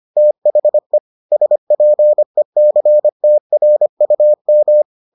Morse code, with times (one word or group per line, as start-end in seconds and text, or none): 0.2-1.0 THE
1.3-4.8 SPECTRUM